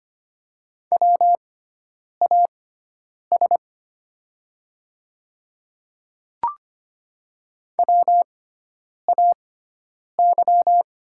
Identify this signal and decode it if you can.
Morse code, heard as a keyed tone, about 25 wpm, spaced as for 9 wpm, 690 Hz. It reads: WAS WAY
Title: 6x Character Spacing - Top 300 Words - 25wpm